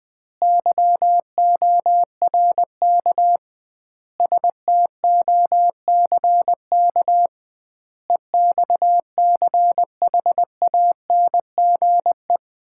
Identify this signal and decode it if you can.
Morse code, heard as a keyed tone, about 20 wpm, 700 Hz. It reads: YORK STOCK EXCHANGE
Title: 2x Word-Spacing - Sets of 5 Words - 20wpm